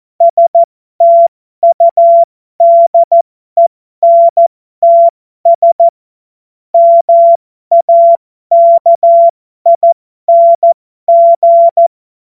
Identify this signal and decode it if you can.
Morse code, heard as a keyed tone, about 14 wpm, spaced as for 12 wpm, 680 Hz. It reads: STUDENTS MAKING
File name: Morsekurs / Witze